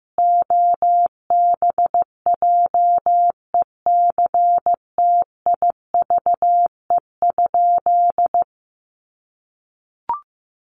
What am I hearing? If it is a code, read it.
Morse code, heard as a keyed tone, about 15 wpm, 700 Hz. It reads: OBJECTIVE?
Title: Sentences from Top 1200 Words 15wpm